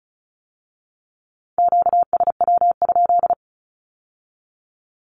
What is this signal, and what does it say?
Morse code, heard as a keyed tone, about 36 wpm, 700 Hz. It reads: QSW?
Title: Q-Codes with Spoken Meaning - 35wpm